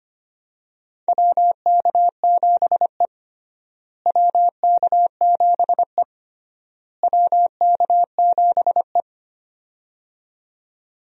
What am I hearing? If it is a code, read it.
Morse code, heard as a keyed tone, about 25 wpm, 700 Hz. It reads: WK7E WK7E WK7E
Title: USA Call Signs - 4 Characters - Repeated 3x - 25wpm